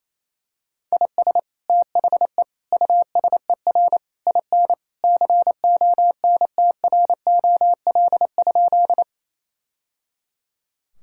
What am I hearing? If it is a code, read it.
Morse code, heard as a keyed tone, about 28 wpm, 700 Hz. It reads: IS THE USER IN CONTROL?
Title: Sentences from Top 1000 Words - Review Part 5 - 28wpm